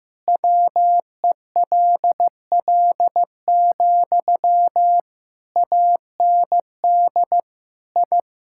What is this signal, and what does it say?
Morse code, heard as a keyed tone, about 15 wpm, 700 Hz. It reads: WELL, AND I
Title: Sentences from Top 1000 Words - Review Part 6 - 15wpm